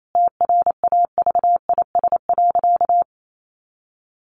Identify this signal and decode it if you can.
Morse code, heard as a keyed tone, about 28 wpm, 700 Hz. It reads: TRAVIS.
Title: A Sound Of Thunder 28wpm